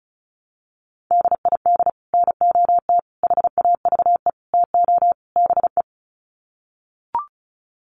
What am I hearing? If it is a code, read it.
Morse code, heard as a keyed tone, about 34 wpm, 700 Hz. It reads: DID NOT HAVE TO BE
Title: Sets of 5 Words Encore2 35wpm